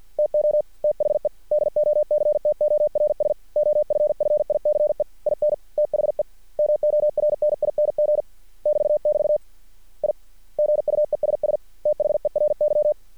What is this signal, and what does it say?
Morse code, heard as a keyed tone, about 50 wpm, 600 Hz. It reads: TO THE DOCTORS OFFICE IN THE MORNING <BT> <BT> I GUESS THERY
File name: archived / CRSnet